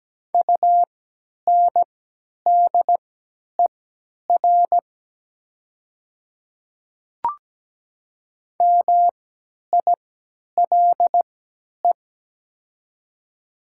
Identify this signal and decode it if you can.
Morse code, heard as a keyed tone, about 17 wpm, spaced as for 10 wpm, 700 Hz. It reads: UNDER MILE
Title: 3x Character Spacing - Top 300 Words - 17wpm